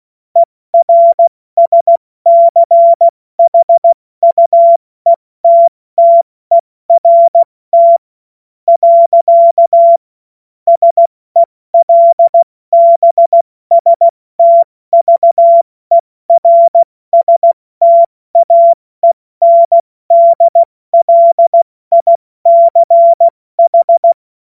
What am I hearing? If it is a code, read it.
Morse code, heard as a keyed tone, about 16 wpm, spaced as for 14 wpm, 680 Hz. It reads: ERSCHUETTERT. SELBSTVERSTAENDLICH
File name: Morsekurs / Witze / Witz 06. September 2022 70 F80